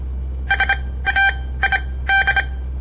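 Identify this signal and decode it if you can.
Morse code, heard as a keyed tone, about 26 wpm, spaced as for 17 wpm, 1550 Hz. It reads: SAID
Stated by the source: On-air Morse code practice broadcast by a southern Montana amateur radio association; all code, no read back.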